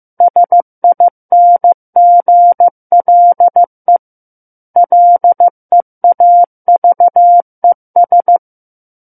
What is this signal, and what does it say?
Morse code, heard as a keyed tone, about 15 wpm, 700 Hz. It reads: SINGLE LEAVES
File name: Book One / 06-Chapter 6 - 15 WPM